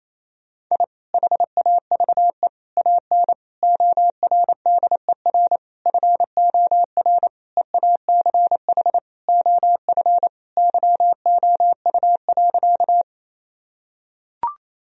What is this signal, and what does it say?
Morse code, heard as a keyed tone, about 28 wpm, 700 Hz. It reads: I HAVE AN ORDER FOR EACH OF YOU.
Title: Sentences from Top 1000 Words - Review Part 2 - 28wpm